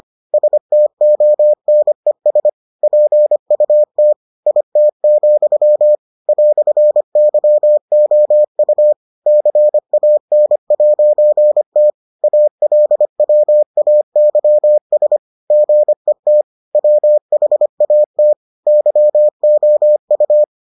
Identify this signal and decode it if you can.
Morse code, heard as a keyed tone, about 25 wpm, 600 Hz. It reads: STONES PUT IT, "YOU CAN'T ALWAYS GET WHAT YOU